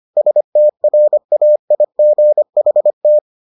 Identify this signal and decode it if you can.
Morse code, heard as a keyed tone, about 25 wpm, 600 Hz. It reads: STRAIGHT